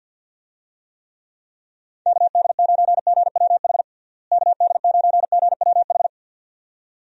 Morse code, heard as a keyed tone, about 47 wpm, 700 Hz. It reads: KD9GWH KD9GWH
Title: USA Call Signs - 6 Characters - Repeated 2x - 50wpm